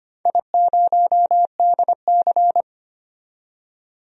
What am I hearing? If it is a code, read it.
Morse code, heard as a keyed tone, about 25 wpm, 700 Hz. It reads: I0DC
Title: DXCC-100 Call Signs - 4 Characters - 25wpm